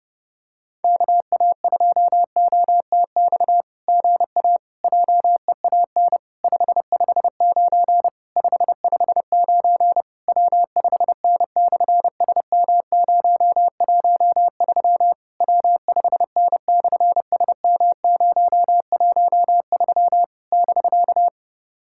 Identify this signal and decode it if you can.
Morse code, heard as a keyed tone, about 30 wpm, 700 Hz. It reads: KA2OTX GA JEAN 559 559 W5N/SM013 W5N/SM013 <BK>
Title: SOTA QSO Practice 30wpm